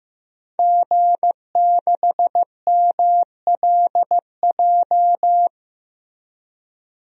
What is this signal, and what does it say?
Morse code, heard as a keyed tone, about 15 wpm, 700 Hz. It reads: G6MLJ
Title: DXCC-100 Call Signs - 5 Characters - Repeated 2x - 15wpm